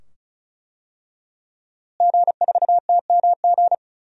Morse code, heard as a keyed tone, about 34 wpm, 700 Hz. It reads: G4TMG
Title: DXCC-100 Call Signs - 5 Characters - Repeated 2x - 35wpm